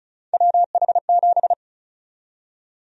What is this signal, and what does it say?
Morse code, heard as a keyed tone, about 34 wpm, 700 Hz. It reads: WH7
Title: USA Call Sign Prefixes - Repeated 2x - 35wpm